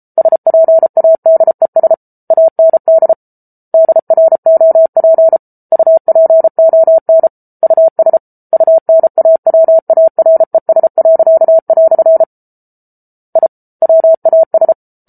Morse code, heard as a keyed tone, about 34 wpm, 650 Hz. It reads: SPADES AND DROP UPON US UNAWARES." I WAS